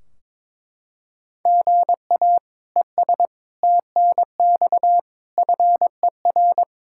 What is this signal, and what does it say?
Morse code, heard as a keyed tone, about 22 wpm, 700 Hz. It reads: GA ES TNX FER
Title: Practice QSO - Exchange 1 - Beginning - 22wpm